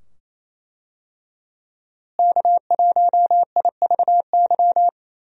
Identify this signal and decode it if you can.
Morse code, heard as a keyed tone, about 27 wpm, 700 Hz. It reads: K1IVY
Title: USA Call Signs - 5 Characters - Repeated 2x - 28wpm